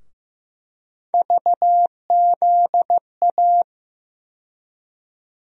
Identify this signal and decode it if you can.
Morse code, heard as a keyed tone, about 15 wpm, 700 Hz. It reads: VZA
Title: USA Call Sign Postfixes - 3 Letters 15wpm